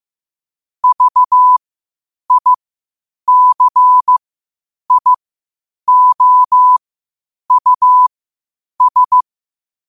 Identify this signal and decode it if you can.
Morse code, heard as a keyed tone, about 15 wpm, spaced as for 8 wpm, 1000 Hz. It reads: VICIOUS
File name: qotdfiles / 8wpm